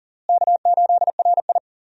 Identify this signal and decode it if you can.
Morse code, heard as a keyed tone, about 40 wpm, 700 Hz. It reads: K8RI
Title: USA Call Signs - 4 Characters - Repeated 3x - 40wpm